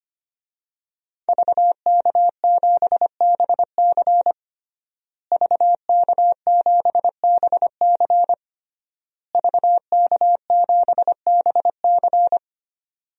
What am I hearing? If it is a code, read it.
Morse code, heard as a keyed tone, about 25 wpm, 700 Hz. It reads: VK7BC VK7BC VK7BC
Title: DXCC-100 Call Signs - 5 Characters - Repeated 3x - 25wpm